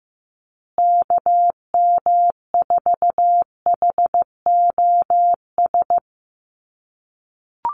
Morse code, heard as a keyed tone, about 15 wpm, 700 Hz. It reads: KM4HOS
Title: USA Call Signs Repeated 3x 15wpm